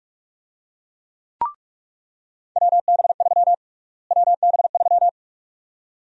Morse code, heard as a keyed tone, about 46 wpm, 700 Hz. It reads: WB3 WB3
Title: USA Call Sign Prefixes - Repeated 2x - 45wpm